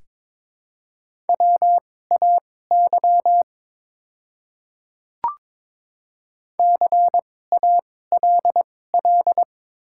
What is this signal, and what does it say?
Morse code, heard as a keyed tone, about 22 wpm, spaced as for 16 wpm, 700 Hz. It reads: WAY CALL
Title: 2x Character Spacing - Top 100 Words - 22wpm